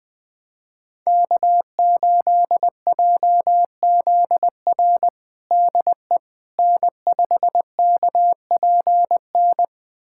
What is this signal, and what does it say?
Morse code, heard as a keyed tone, about 20 wpm, 700 Hz. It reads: K8JZR DE N5KPN
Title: USA Callsign DE Callsign - 5 Characters - 20wpm